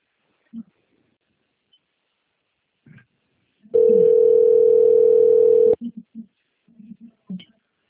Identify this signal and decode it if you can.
Ringback tone (North America), 440 + 480 Hz.